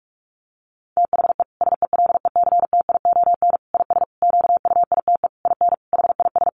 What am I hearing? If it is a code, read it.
Morse code, heard as a keyed tone, about 44 wpm, 700 Hz. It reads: THE SELECTION IS QUITE IN HIS
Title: Sentences from Top 1400 Words 45wpm